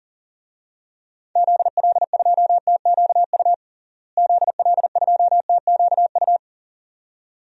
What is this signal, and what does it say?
Morse code, heard as a keyed tone, about 40 wpm, 700 Hz. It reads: ZL2TQU ZL2TQU
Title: DXCC-100 Call Signs - 6 Characters - Repeated 2x - 40wpm